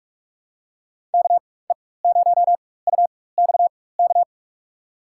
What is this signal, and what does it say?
Morse code, heard as a keyed tone, about 46 wpm, spaced as for 21 wpm, 700 Hz. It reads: KE0UXK
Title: USA Call Signs - 6 Characters - 4x Character Spacing - 45wpm